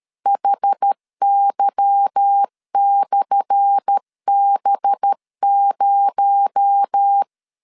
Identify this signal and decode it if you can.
Morse code, heard as a keyed tone, about 13 wpm, 800 Hz. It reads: HY/B0
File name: LEZIONI / 10 SIGNS / 10A SLASH 12wpm 2314